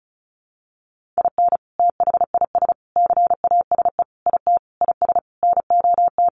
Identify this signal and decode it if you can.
Morse code, heard as a keyed tone, about 34 wpm, 700 Hz. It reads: IN THIS CASE IT IS NOT
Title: Sentences from Top 1400 Words 35wpm